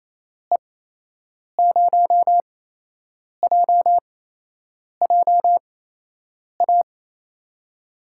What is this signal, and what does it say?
Morse code, heard as a keyed tone, about 28 wpm, spaced as for 8 wpm, 700 Hz. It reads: E0JJA
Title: USA Call Signs - 6 Characters - 8x Character Spacing - 28wpm